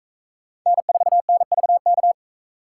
Morse code, heard as a keyed tone, about 44 wpm, 700 Hz. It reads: N4NVK